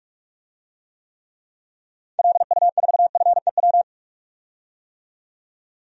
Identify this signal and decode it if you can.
Morse code, heard as a keyed tone, about 46 wpm, 700 Hz. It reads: LU4FEW
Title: DXCC-100 Call Signs - 6 Characters - 45wpm